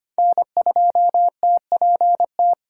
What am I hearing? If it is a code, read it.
Morse code, heard as a keyed tone, about 25 wpm, 700 Hz. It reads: N2TPT
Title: USA Call Signs - 5 Characters - 2x Character Spacing - 25wpm